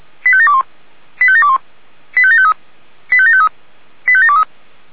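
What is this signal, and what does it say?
Five-tone selective call (ZVEI), heard as quick runs of five tones, tones of about 70 ms. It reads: 85721 85721 85753 85753 85723
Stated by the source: ZVEI five-tone call received by SDR.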